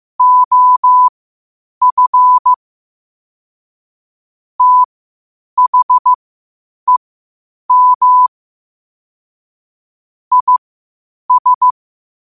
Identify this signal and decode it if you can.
Morse code, heard as a keyed tone, about 15 wpm, spaced as for 8 wpm, 1000 Hz. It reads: OF THEM IS
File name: qotdfiles / 8wpm